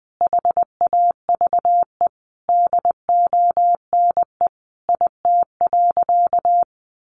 Morse code, heard as a keyed tone, about 20 wpm, 700 Hz. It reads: HAVE DONE IT.
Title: Sentences from Top 1200 Words 20wpm